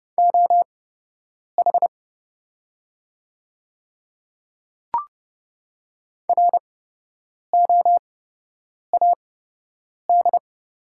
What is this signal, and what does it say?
Morse code, heard as a keyed tone, about 30 wpm, spaced as for 8 wpm, 700 Hz. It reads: OH ROAD
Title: 8x Character Spacing - Top 500 Words - 30wpm